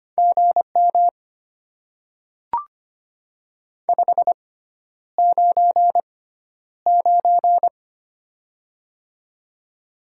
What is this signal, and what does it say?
Morse code, heard as a keyed tone, about 25 wpm, spaced as for 11 wpm, 700 Hz. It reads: GM 599
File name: Top 90 QSO Elements - 6x Character Spacing - 25wpm